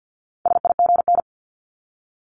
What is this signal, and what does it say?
Morse code, heard as a keyed tone, about 50 wpm, 700 Hz. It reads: SIGN
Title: Multi-letter Phonemes 50wpm